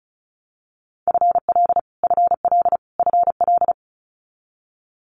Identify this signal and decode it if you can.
Morse code, heard as a keyed tone, about 36 wpm, 700 Hz. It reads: FL FL FL
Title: Multi-letter Phonemes Repeated 3x 35wpm